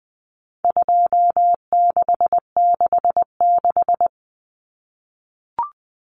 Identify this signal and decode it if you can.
Morse code, heard as a keyed tone, about 20 wpm, 700 Hz. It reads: 2666
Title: Common Cut Numbers - 4-Digit 20wpm